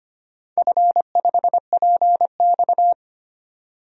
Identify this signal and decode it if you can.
Morse code, heard as a keyed tone, about 25 wpm, 700 Hz. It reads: F5PX
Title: DXCC-100 Call Signs - 4 Characters - 25wpm